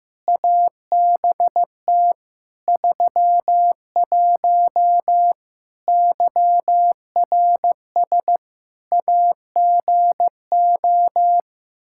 Morse code, heard as a keyed tone, about 15 wpm, 700 Hz. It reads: ABT 31 YRS AGO
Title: K1USN SST QSO Practice 15wpm